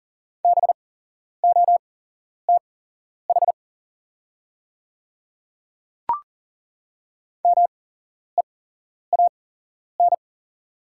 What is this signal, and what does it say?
Morse code, heard as a keyed tone, about 40 wpm, spaced as for 11 wpm, 690 Hz. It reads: BOTH MEAN
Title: 8x Character Spacing - Top 500 Words - 40wpm